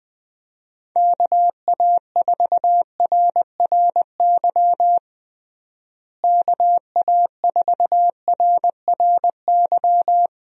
Morse code, heard as a keyed tone, about 20 wpm, 700 Hz. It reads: KA4RRY KA4RRY
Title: USA Call Signs - 6 Characters - Repeated 2x - 20wpm